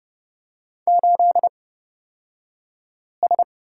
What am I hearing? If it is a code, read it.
Morse code, heard as a keyed tone, about 30 wpm, spaced as for 5 wpm, 700 Hz. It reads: 8S
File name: Lesson 37d - Letter x - Two Characters 30wpm